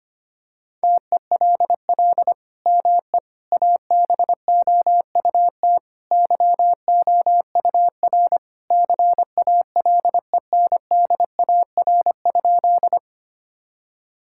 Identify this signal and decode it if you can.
Morse code, heard as a keyed tone, about 25 wpm, 700 Hz. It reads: TELL ME ABOUT YOUR CALENDAR?
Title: Sentences from Top 1600 Words 25wpm